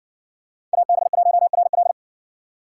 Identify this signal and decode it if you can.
Morse code, heard as a keyed tone, about 60 wpm, 700 Hz. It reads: AB1RL